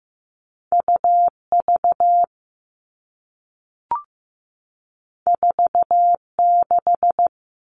Morse code, heard as a keyed tone, about 15 wpm, 700 Hz. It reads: UV 46
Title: All Cut Numbers - 2-Digit 15wpm